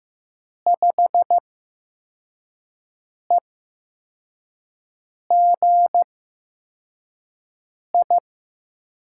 Morse code, heard as a keyed tone, about 15 wpm, spaced as for 4 wpm, 700 Hz. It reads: 5EGI
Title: USA Call Signs - 5 Characters - 8x Character Spacing - 15wpm